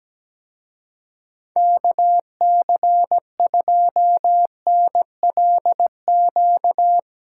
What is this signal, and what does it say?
Morse code, heard as a keyed tone, about 17 wpm, 700 Hz. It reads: KC2NLQ
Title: Speed Racing - USA 2x3 Call Signs 17wpm